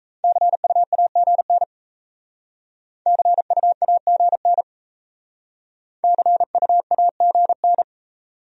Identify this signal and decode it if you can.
Morse code, heard as a keyed tone, about 35 wpm, 700 Hz. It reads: CUAGN CUAGN CUAGN
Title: Speed-Racing Abbreviations - 28wpm